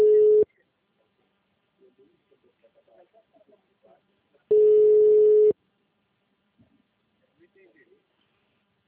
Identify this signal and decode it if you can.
Ringback tone (Europe, 425 Hz), with speech.